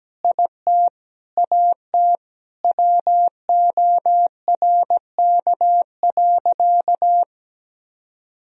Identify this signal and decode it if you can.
Morse code, heard as a keyed tone, about 17 wpm, 700 Hz. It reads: IT AT WORK.
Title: Sentences from Top 1000 Words - Review Part 4 - 17wpm